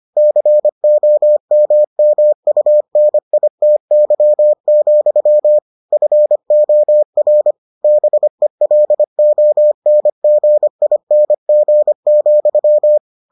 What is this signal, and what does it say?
Morse code, heard as a keyed tone, about 25 wpm, 600 Hz. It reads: COMMUNITY, FOR BELONGING,